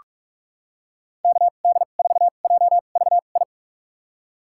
Morse code, heard as a keyed tone, about 46 wpm, spaced as for 32 wpm, 700 Hz. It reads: KD4JVI
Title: USA Call Signs - 6 Characters - 2x Character Spacing - 45wpm